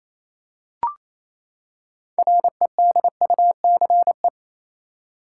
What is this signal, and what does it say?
Morse code, heard as a keyed tone, about 28 wpm, 700 Hz. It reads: REDUCE
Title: Six Letter Words - 28wpm